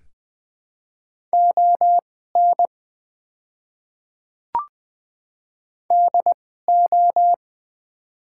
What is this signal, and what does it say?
Morse code, heard as a keyed tone, about 20 wpm, spaced as for 14 wpm, 700 Hz. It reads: ON DO
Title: 2x Character Spacing - Top 200 Words - 20wpm